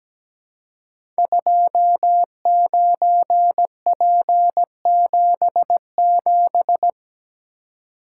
Morse code, heard as a keyed tone, about 17 wpm, 700 Hz. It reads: 29P77
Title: Code Groups - 5 Letter:Numbers - 17wpm